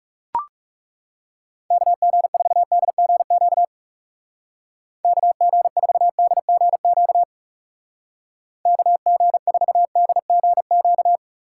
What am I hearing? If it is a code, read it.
Morse code, heard as a keyed tone, about 37 wpm, 700 Hz. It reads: KG4DGQ KG4DGQ KG4DGQ
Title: Speed Racing - USA Call Signs 30wpm